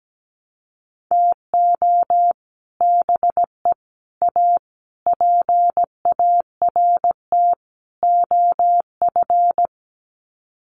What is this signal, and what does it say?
Morse code, heard as a keyed tone, about 17 wpm, 700 Hz. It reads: TO BE A PART OF
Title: Sets of 5 Words 17wpm